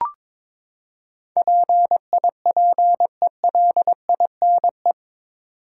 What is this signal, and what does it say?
Morse code, heard as a keyed tone, about 22 wpm, 700 Hz. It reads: PIPELINE
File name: practice / 22wpm / Eight Letter Words - 22wpm